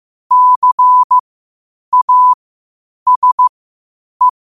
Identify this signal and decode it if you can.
Morse code, heard as a keyed tone, about 15 wpm, spaced as for 8 wpm, 1000 Hz. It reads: CASE
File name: qotdfiles / 8wpm